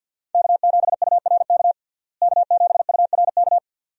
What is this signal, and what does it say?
Morse code, heard as a keyed tone, about 51 wpm, 700 Hz. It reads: K7URK K7URK
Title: USA Call Signs - 5 Characters - Repeated 2x - 50wpm